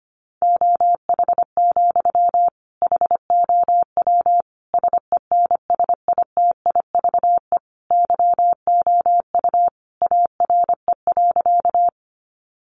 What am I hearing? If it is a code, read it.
Morse code, heard as a keyed tone, about 25 wpm, 700 Hz. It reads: OH, HOW SENSITIVE YOU ARE.
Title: Sentences from Top 1400 Words 25wpm